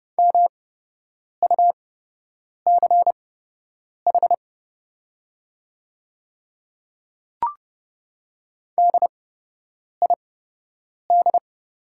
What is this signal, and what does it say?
Morse code, heard as a keyed tone, about 29 wpm, spaced as for 8 wpm, 690 Hz. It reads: MUCH DID
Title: 8x Character Spacing - Top 200 Words - 30wpm